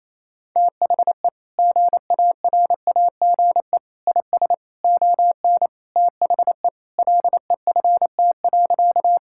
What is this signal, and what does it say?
Morse code, heard as a keyed tone, about 28 wpm, 700 Hz. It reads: THE GARAGE IS ON THE LEFT.
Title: Sentences from Top 1600 Words 28wpm